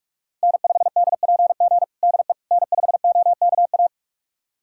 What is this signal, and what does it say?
Morse code, heard as a keyed tone, about 46 wpm, 700 Hz. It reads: N5DPG DE N5OKA
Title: USA Callsign DE Callsign - 5 Characters - 45wpm